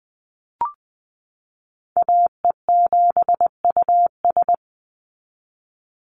Morse code, heard as a keyed tone, about 20 wpm, 700 Hz. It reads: AE7US
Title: Call Signs Repeated 3x - 20wpm